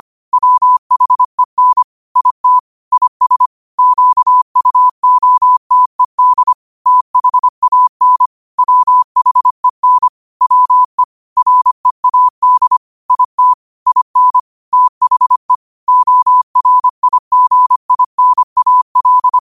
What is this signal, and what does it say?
Morse code, heard as a keyed tone, about 25 wpm, 1000 Hz. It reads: WHEN IT IS QUOTED THAN WHEN WE READ IT IN THE ORIGINAL
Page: Quotes for Wed, 13 Aug 2025 in Morse Code at 25 words per minute.